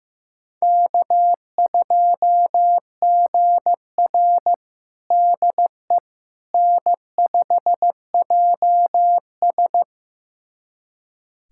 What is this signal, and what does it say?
Morse code, heard as a keyed tone, about 15 wpm, 700 Hz. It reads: K2GR DE N5JS
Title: USA Callsign DE Callsign - 4 Characters - 15wpm